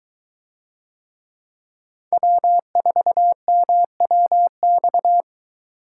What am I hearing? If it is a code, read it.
Morse code, heard as a keyed tone, about 20 wpm, 700 Hz. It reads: W4MWX